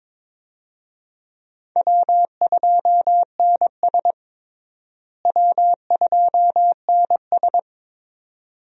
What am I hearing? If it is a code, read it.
Morse code, heard as a keyed tone, about 22 wpm, 700 Hz. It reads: W2NS W2NS
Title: USA Call Signs - 4 Characters - Repeated 2x - 22wpm